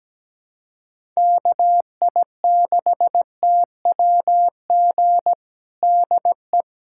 Morse code, heard as a keyed tone, about 17 wpm, 700 Hz. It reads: KI6TWG DE
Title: USA Callsign DE Callsign - 6 Characters - 17wpm